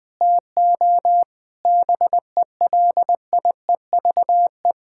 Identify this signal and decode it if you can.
Morse code, heard as a keyed tone, about 20 wpm, 700 Hz. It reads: TO BELIEVE
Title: Sentences with 7-Word Phrases - 20wpm